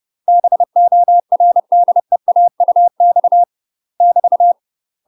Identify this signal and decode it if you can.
Morse code, heard as a keyed tone, about 30 wpm, 700 Hz. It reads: BORDEAUX <BT>
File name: UK MASTER QSOs TRAINING / 30WPM / UK MASTER QSOs 0047